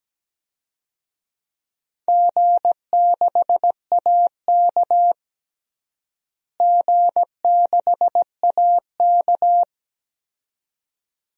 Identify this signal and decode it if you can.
Morse code, heard as a keyed tone, about 17 wpm, 700 Hz. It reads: G6AK G6AK
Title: DXCC-100 Call Signs - 4 Characters - Repeated 2x - 17wpm